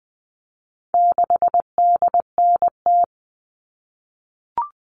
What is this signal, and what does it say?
Morse code, heard as a keyed tone, about 20 wpm, 700 Hz. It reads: 6DNT
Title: All Cut Numbers - 4-Digit 20wpm